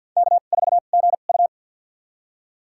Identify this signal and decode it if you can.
Morse code, heard as a keyed tone, about 53 wpm, spaced as for 37 wpm, 700 Hz. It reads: K4GU
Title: USA Call Signs - 4 Characters - 2x Character Spacing - 50wpm